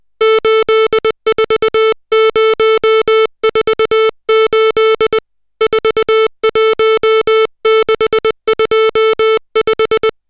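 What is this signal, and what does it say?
Morse code, heard as a keyed tone, about 20 wpm, 440 Hz, with noise. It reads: 84048 41625
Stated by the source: Morse code drill of digits at 60 characters per minute.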